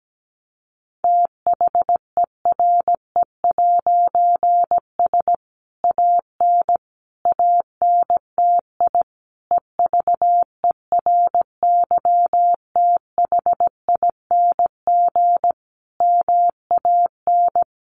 Morse code, heard as a keyed tone, about 17 wpm, 700 Hz. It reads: THERE'S AN ANTI EVERYTHING MAN
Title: A Sound Of Thunder 17wpm